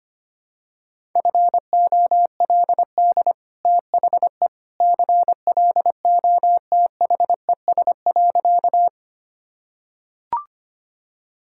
Morse code, heard as a keyed tone, about 25 wpm, 690 Hz. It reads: FOLD THE CLOTHES.
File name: Easy Sentences with Hints - 25wpm